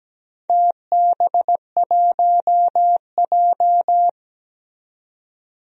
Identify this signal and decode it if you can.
Morse code, heard as a keyed tone, about 17 wpm, 700 Hz. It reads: TB1J
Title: DXCC-100 Call Signs - 4 Characters - 17wpm